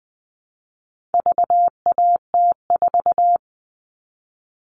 Morse code, heard as a keyed tone, about 20 wpm, 700 Hz. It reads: VAT4
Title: All Cut Numbers - 4-Digit 20wpm